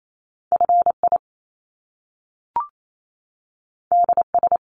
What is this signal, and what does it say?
Morse code, heard as a keyed tone, about 28 wpm, 700 Hz. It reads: FI DS
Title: Top Two Letter Combinations in Words 28wpm